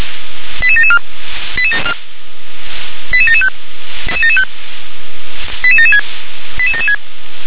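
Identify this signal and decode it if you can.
Five-tone selective call (ZVEI), heard as quick runs of five tones, tones of about 70 ms. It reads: 88774 88775 88776 88776
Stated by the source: ZVEI five-tone selective call received on an SDR.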